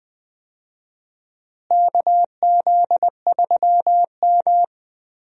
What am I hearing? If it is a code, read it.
Morse code, heard as a keyed tone, about 20 wpm, 700 Hz. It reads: KZ3M
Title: USA Call Signs - 4 Characters - Repeated 2x - 20wpm